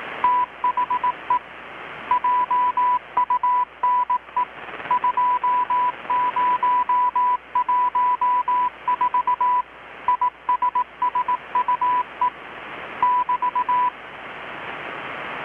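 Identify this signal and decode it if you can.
Morse code, heard as a keyed tone, about 18 wpm, 990 Hz. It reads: THE JUNE 2014 ISSUE <BT>
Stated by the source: Morse from KPH's Night of Nights XX maritime radiotelegraphy event, received in Chiapas, Mexico about 3,800 km away on a Kenwood R-600 with a grounded 10 m long-wire antenna.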